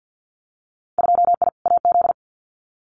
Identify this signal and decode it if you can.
Morse code, heard as a keyed tone, about 50 wpm, 700 Hz. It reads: 2I AZ
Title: Winter Field Day Exchanges 50wpm